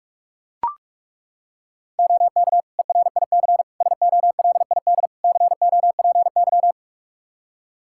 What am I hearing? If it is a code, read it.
Morse code, heard as a keyed tone, about 42 wpm, 700 Hz. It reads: OK ERIC SOLID COPY
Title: Practice QSO - Exchange 3 - More Info - 45wpm